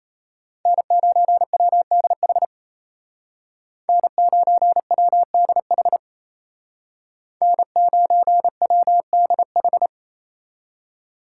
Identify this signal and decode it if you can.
Morse code, heard as a keyed tone, about 33 wpm, 700 Hz. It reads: N9WDH N9WDH N9WDH
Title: Speed Racing - USA 1x3 Call Signs 28wpm